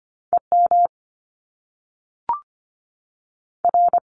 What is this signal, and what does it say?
Morse code, heard as a keyed tone, about 25 wpm, 700 Hz. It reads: E M R
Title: Top Two Letter Combinations in Words 25wpm